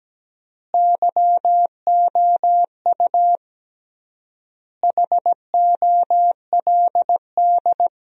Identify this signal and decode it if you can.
Morse code, heard as a keyed tone, about 17 wpm, 700 Hz. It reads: YOU HOLD
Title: 3x Word-Spacing - Sentences from Top 400 Words - 17wpm